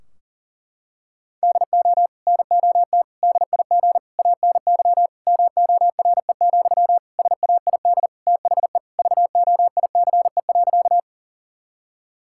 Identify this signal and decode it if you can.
Morse code, heard as a keyed tone, about 37 wpm, 700 Hz. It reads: DO NOT DIG ANY MORE, SAID THE VOICE.
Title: Sentences from Top 1600 Words 40wpm